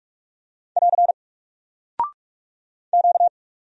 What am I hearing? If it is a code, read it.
Morse code, heard as a keyed tone, about 44 wpm, 700 Hz. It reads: <AR> Q
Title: Single Character (Letter:Number:Punctuation:Prosign) Alt 45wpm